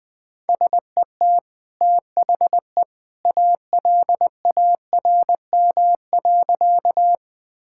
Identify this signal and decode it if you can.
Morse code, heard as a keyed tone, about 20 wpm, 700 Hz. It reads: SET THE ALARM.
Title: Easy Sentences with Hints - 20wpm